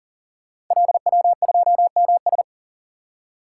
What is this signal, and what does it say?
Morse code, heard as a keyed tone, about 38 wpm, 690 Hz. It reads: LW2MS
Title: Code Groups - 5 Letter:Numbers - 40wpm